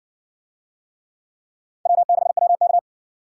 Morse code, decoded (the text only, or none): W6WK